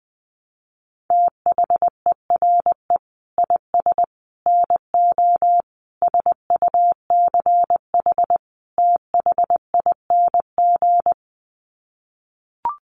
Morse code, heard as a keyed tone, about 20 wpm, 700 Hz. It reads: THERE IS NO SUCH THING
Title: Sets of 5 Words Encore 20wpm